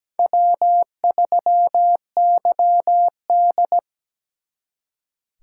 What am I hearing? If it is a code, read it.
Morse code, heard as a keyed tone, about 20 wpm, 700 Hz. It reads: W3YD
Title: Speed Racing - USA Call Signs 17wpm